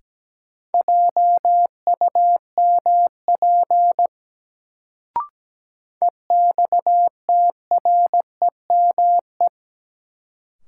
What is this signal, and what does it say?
Morse code, heard as a keyed tone, about 17 wpm, 700 Hz. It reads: JUMP EXTREME
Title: Sentences from Top 1300 Words - New Word Drill Alt - 17wpm